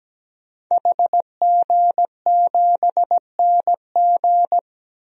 Morse code, heard as a keyed tone, about 17 wpm, 700 Hz. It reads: HG7NG
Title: DXCC-100 Call Signs - 5 Characters - Repeated 2x - 17wpm